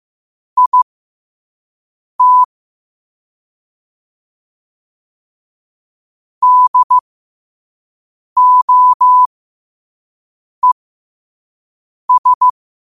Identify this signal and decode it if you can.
Morse code, heard as a keyed tone, about 15 wpm, spaced as for 5 wpm, 1000 Hz. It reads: IT DOES
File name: qotdfiles / 5wpm